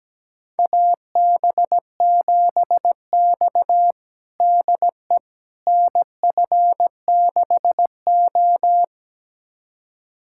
Morse code, heard as a keyed tone, about 17 wpm, 700 Hz. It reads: AB7X DE NF6O
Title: USA Callsign DE Callsign - 4 Characters - 17wpm